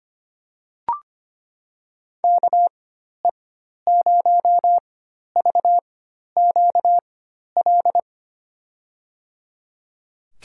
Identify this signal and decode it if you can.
Morse code, heard as a keyed tone, about 25 wpm, spaced as for 12 wpm, 700 Hz. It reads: KE0VQL
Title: USA Call Signs - 6 Characters - 4x Character Spacing - 25wpm